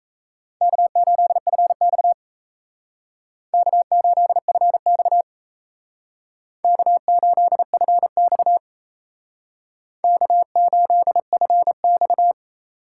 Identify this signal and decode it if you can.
Morse code, heard as a keyed tone, about 36 wpm, 700 Hz. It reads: K8FX K8FX K8FX K8FX